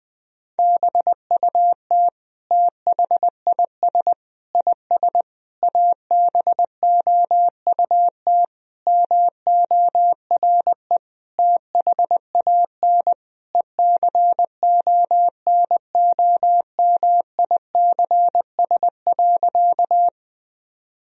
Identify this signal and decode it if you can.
Morse code, heard as a keyed tone, about 20 wpm, 700 Hz. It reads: BUT THIS IS ABOUT MORE THAN ECONOMICS.
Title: Sentences from Top 1000 Words - Review Part 3 - 20wpm